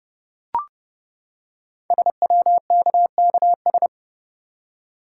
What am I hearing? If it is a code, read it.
Morse code, heard as a keyed tone, about 30 wpm, 700 Hz. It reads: SWKKS
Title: Code Groups - 5 Letters - 30wpm